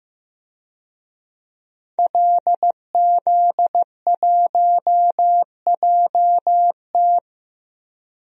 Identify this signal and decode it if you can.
Morse code, heard as a keyed tone, about 16 wpm, 700 Hz. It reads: LZ1JT
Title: DXCC-100 Call Signs - 5 Characters - Repeated 3x - 15wpm